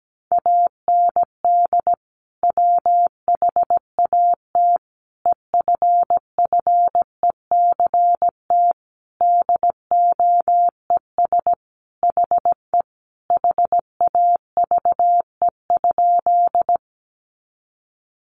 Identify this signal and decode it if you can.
Morse code, heard as a keyed tone, about 17 wpm, 700 Hz. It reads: AND WHAT EFFECT DOES HE HAVE?
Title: Sentences from Top 1000 Words 17wpm